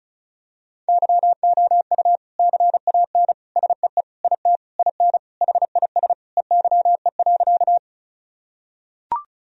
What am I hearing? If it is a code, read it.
Morse code, heard as a keyed tone, about 34 wpm, 700 Hz. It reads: YOU CAN SEE IT IN HIS EYE.
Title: Sentences from Top 1000 Words - Review Part 1 - 35wpm